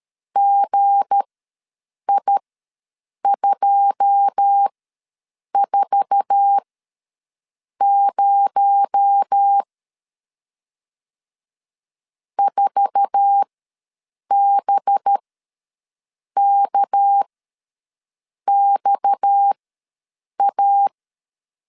Morse code, heard as a keyed tone, about 13 wpm, spaced as for 6 wpm, 800 Hz. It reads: GI240 4BKXA
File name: LEZIONI / 9 286 ws 12wpm 3351